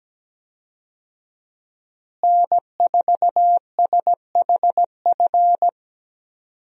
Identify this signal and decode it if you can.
Morse code, heard as a keyed tone, about 17 wpm, 700 Hz. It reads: N4SHF
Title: USA Call Signs - 5 Characters - Repeated 2x - 17wpm